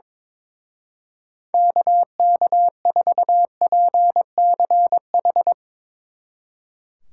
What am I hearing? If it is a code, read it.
Morse code, heard as a keyed tone, about 22 wpm, 700 Hz. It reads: KK4PCH
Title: Speed Racing - USA 2x3 Call Signs 22wpm